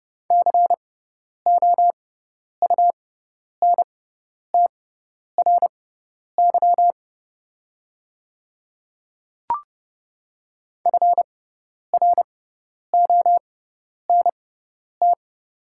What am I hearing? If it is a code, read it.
Morse code, heard as a keyed tone, about 30 wpm, spaced as for 10 wpm, 690 Hz. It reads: COUNTRY FRONT
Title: 6x Character Spacing - Top 500 Words - 30wpm